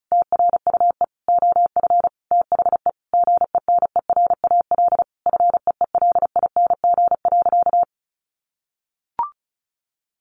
Morse code, heard as a keyed tone, about 34 wpm, 700 Hz. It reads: TRUE OF THE GENERAL FEELING.
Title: Sentences from Top 1400 Words 35wpm